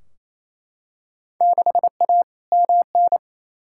Morse code, heard as a keyed tone, about 28 wpm, 700 Hz. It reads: 6A MN
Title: Field Day Exchanges Encore - 28wpm